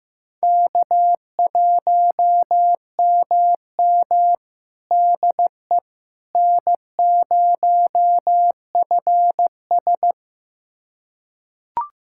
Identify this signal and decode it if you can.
Morse code, heard as a keyed tone, about 15 wpm, 700 Hz. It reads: K1MM DE N0FS
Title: USA Callsign DE Callsign - 4 Characters - 15wpm